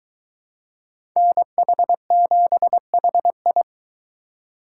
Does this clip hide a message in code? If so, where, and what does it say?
Yes: Morse code from 1.2 to 3.6 s, heard as a keyed tone, about 23 wpm, 700 Hz. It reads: NH7HI